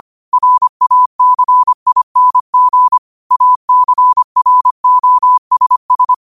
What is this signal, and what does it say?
Morse code, heard as a keyed tone, about 25 wpm, 1000 Hz. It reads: RACING ACROSS
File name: 25wpm